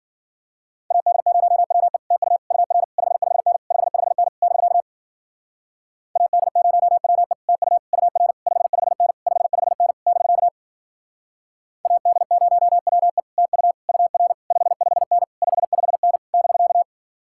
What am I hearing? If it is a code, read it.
Morse code, heard as a keyed tone, about 51 wpm, 700 Hz. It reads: AD0WE TU UR 55N 55N <BK> AD0WE TU UR 55N 55N <BK> AD0WE TU UR 55N 55N <BK>